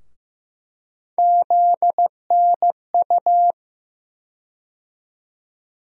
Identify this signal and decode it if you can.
Morse code, heard as a keyed tone, about 15 wpm, 700 Hz. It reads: ZNU
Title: USA Call Sign Postfixes - 3 Letters 15wpm